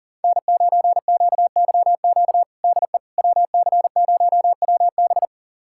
Morse code, heard as a keyed tone, about 40 wpm, 700 Hz. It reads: N9QYQ DE WC0WB
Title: USA Callsign DE Callsign - 5 Characters - 40wpm